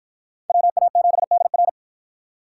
Morse code, heard as a keyed tone, about 44 wpm, 700 Hz. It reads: WA7DR